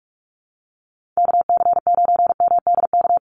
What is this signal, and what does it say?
Morse code, heard as a keyed tone, about 44 wpm, 700 Hz. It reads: KC9MDK
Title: USA Call Signs 45wpm